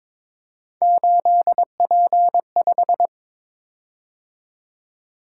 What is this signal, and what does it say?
Morse code, heard as a keyed tone, about 22 wpm, 700 Hz. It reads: 8P5
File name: practice / 22wpm / Three Letter-Number Combinations - 22wpm